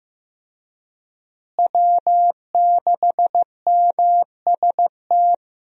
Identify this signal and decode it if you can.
Morse code, heard as a keyed tone, about 15 wpm, 700 Hz. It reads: W6MST
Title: USA Call Signs - 5 Characters - Repeated 3x - 15wpm